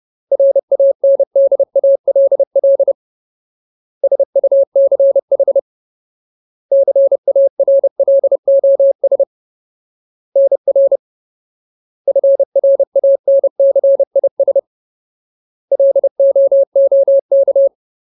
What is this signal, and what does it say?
Morse code, heard as a keyed tone, about 30 wpm, 550 Hz. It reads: RANDALL SUCH CARLOS NR FRANCIS LOOK
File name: Ultimate Word List / 30WPM Ultimate